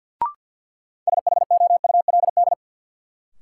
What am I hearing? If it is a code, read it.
Morse code, heard as a keyed tone, about 53 wpm, 700 Hz. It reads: SHOULD
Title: Sentences from Top 300 Words - New Word Drill - 50wpm